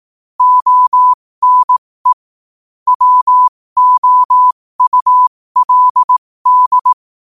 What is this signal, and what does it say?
Morse code, heard as a keyed tone, about 18 wpm, spaced as for 15 wpm, 1000 Hz. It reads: ONE WOULD
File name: qotdfiles / 15wpm